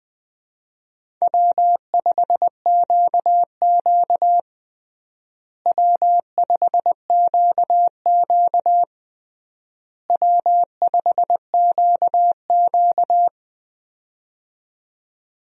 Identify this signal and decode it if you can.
Morse code, heard as a keyed tone, about 20 wpm, 700 Hz. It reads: W5QQ W5QQ W5QQ
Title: USA Call Signs - 4 Characters - Repeated 3x - 20wpm